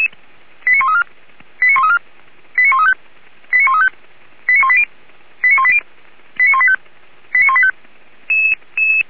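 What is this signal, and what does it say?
Five-tone selective call (ZVEI), heard as quick runs of five tones, tones of about 70 ms. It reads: 89134 89134 89135 89135 89180 89180 89185 89185
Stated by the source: ZVEI five-tone call received by SDR, recorded for Feuerwehr Orlamünde.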